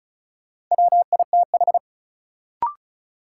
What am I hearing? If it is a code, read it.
Morse code, heard as a keyed tone, about 36 wpm, 700 Hz. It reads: WITH
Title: Sets of 2 Words - New Word Drill - 35wpm